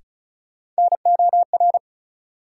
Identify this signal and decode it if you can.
Morse code, heard as a keyed tone, about 36 wpm, 700 Hz. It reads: NOR
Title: Sentences from Top 1500 Words - New Word Drill Alt - 35wpm